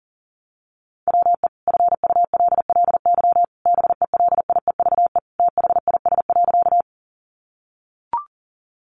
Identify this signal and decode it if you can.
Morse code, heard as a keyed tone, about 40 wpm, 700 Hz. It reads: WE FULLY BELIEVE THIS.
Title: Sentences from Top 1300 Words 40wpm